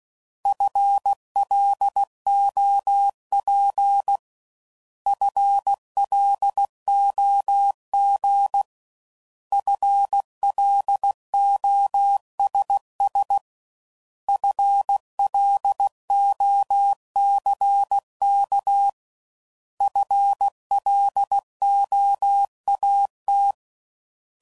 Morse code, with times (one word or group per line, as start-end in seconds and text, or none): 0.5-4.2 FLOP
5.1-8.6 FLOG
9.5-13.4 FLOSS
14.3-18.9 FLOCK
19.8-23.5 FLOAT